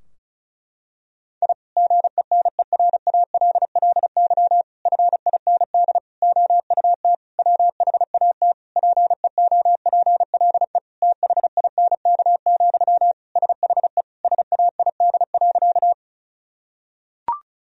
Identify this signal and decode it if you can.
Morse code, heard as a keyed tone, about 34 wpm, 700 Hz. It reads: I GENERALLY FIND OUT WHAT PEOPLE THINK, SHE SAID.
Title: Sentences from Top 1000 Words - Review Part 4 - 35wpm